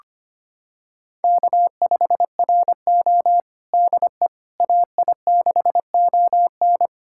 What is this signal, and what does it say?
Morse code, heard as a keyed tone, about 25 wpm, 700 Hz. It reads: K5RO DE AI6ON
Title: USA Callsign DE Callsign 25wpm